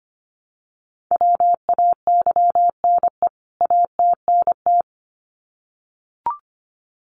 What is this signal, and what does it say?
Morse code, heard as a keyed tone, about 25 wpm, 700 Hz. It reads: WAYNE ATNT
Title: CWT Exchange 25wpm